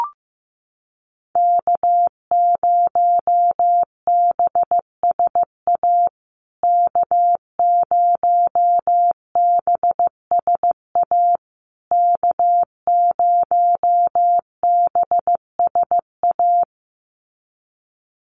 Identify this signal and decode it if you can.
Morse code, heard as a keyed tone, about 15 wpm, 700 Hz. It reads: K0BSA K0BSA K0BSA
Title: USA Call Signs Repeated 3x 15wpm